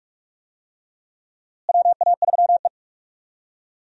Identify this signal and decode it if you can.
Morse code, heard as a keyed tone, about 46 wpm, 700 Hz. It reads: WA3E
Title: USA Call Signs - 4 Characters - Repeated 2x - 45wpm